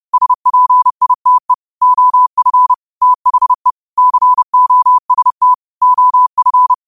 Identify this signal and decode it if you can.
Morse code, heard as a keyed tone, about 30 wpm, 1000 Hz. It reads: SPITE OF THE COST OF
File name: 30wpm